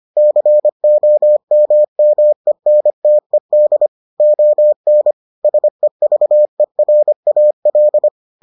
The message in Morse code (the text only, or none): COMMENTED ON SEVERAL